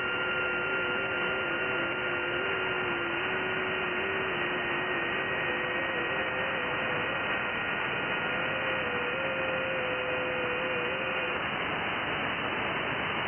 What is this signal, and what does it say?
FT8.
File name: IC-7100